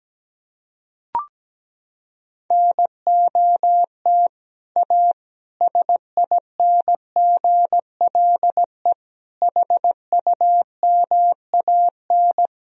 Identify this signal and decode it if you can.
Morse code, heard as a keyed tone, about 17 wpm, 700 Hz. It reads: NOT A SINGLE HUMAN
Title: Sentences from Top 1000 Words - Review Part 4 - 17wpm